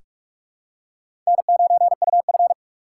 Morse code, heard as a keyed tone, about 44 wpm, 700 Hz. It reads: N9UF
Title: USA Call Signs - 6x Character Spacing - 45wpm